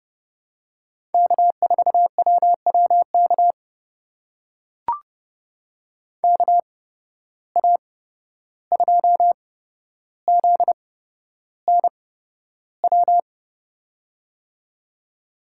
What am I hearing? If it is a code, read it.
Morse code, heard as a keyed tone, about 30 wpm, spaced as for 12 wpm, 700 Hz. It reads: K4WWK KA2ZNW
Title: USA Call Signs - 8x Character Spacing - 30wpm